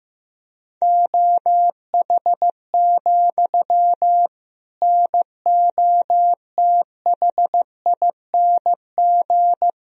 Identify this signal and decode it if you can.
Morse code, heard as a keyed tone, about 15 wpm, 700 Hz. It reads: OH, NOTHING
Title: Sentences from Top 1000 Words - Review Part 3 - 15wpm